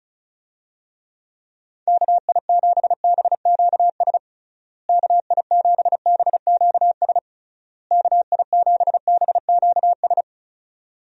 Morse code, heard as a keyed tone, about 34 wpm, 700 Hz. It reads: KI7BQS KI7BQS KI7BQS
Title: USA Call Signs - 6 Characters - Repeated 3x - 35wpm